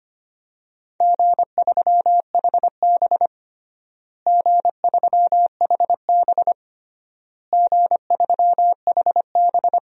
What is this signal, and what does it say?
Morse code, heard as a keyed tone, about 25 wpm, 700 Hz. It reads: G3HB G3HB G3HB
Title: DXCC-100 Call Signs - 4 Characters - Repeated 3x - 25wpm